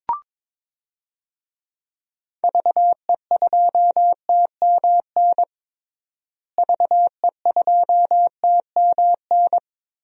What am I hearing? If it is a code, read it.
Morse code, heard as a keyed tone, about 22 wpm, 700 Hz. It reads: VE2TMN VE2TMN
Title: DXCC-100 Call Signs - 6 Characters - Repeated 2x - 22wpm